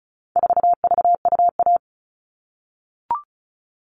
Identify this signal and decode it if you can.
Morse code, heard as a keyed tone, about 34 wpm, 700 Hz. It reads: 4VUA
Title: All Cut Numbers - 4-Digit 35wpm